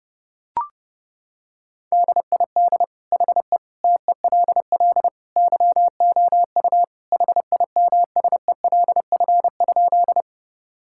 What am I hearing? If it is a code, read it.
Morse code, heard as a keyed tone, about 30 wpm, 700 Hz. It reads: DID HE TELL YOU HIMSELF?
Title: Sentences from Top 1600 Words 30wpm